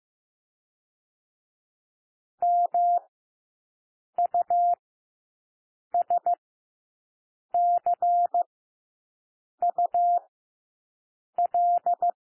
Morse code, heard as a keyed tone, about 15 wpm, 700 Hz. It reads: M U S C U L
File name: cwmp3 / W1 SOUND WORDS 15 wpm